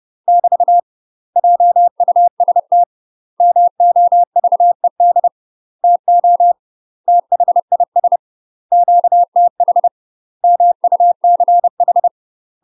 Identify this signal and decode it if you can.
Morse code, heard as a keyed tone, about 30 wpm, 700 Hz. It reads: <BT> JUST MOVED TO THIS QTH MUCH
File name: UK MASTER QSOs TRAINING / 30WPM / UK MASTER QSOs 0051